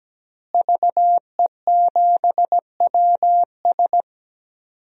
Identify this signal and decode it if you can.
Morse code, heard as a keyed tone, about 17 wpm, 700 Hz. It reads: VE7WS
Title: DXCC-100 Call Signs - 5 Characters - Repeated 2x - 17wpm